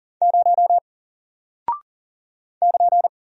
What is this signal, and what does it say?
Morse code, heard as a keyed tone, about 40 wpm, 700 Hz. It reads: 0 <KN>
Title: Single Character (Letter:Number:Punctuation:Prosign) Alt 40wpm